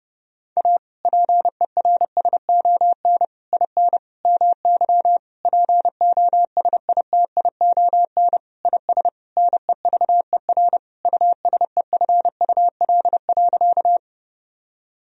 Morse code, heard as a keyed tone, about 30 wpm, 700 Hz. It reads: A PERSON IN MY POSITION IS NEVER USEFUL.
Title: Sentences from Top 1000 Words - Review Part 4 - 30wpm